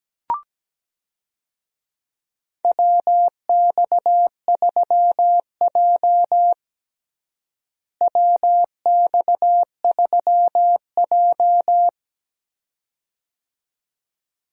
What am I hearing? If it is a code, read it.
Morse code, heard as a keyed tone, about 17 wpm, 700 Hz. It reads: WX3J WX3J
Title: USA Call Signs - 4 Characters - Repeated 2x - 17wpm